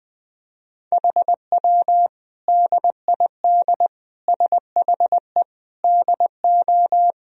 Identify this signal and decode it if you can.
Morse code, heard as a keyed tone, about 21 wpm, 700 Hz. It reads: HW DID SHE DO
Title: Sentences from Top 100 Words and Abbreviations 20wpm